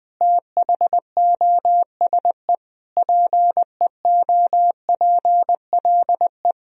Morse code, heard as a keyed tone, about 21 wpm, 700 Hz. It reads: THOSE PEOPLE
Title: Sentences from Top 1000 Words - Review Part 1 - 20wpm